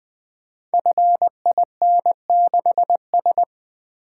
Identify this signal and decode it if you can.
Morse code, heard as a keyed tone, about 20 wpm, 700 Hz. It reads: FIN6S
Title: Code Groups - 5 Letter:Numbers - 20wpm